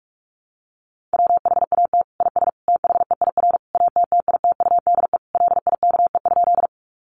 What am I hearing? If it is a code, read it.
Morse code, heard as a keyed tone, about 44 wpm, 700 Hz. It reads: WHAT IS THEIR ATTITUDE LIKE?
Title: Sentences from Top 1300 Words 45wpm